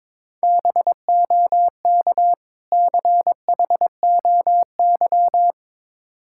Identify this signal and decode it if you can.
Morse code, heard as a keyed tone, about 23 wpm, 690 Hz. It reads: BOK CHOY
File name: Difficult Words with Hints - 22wpm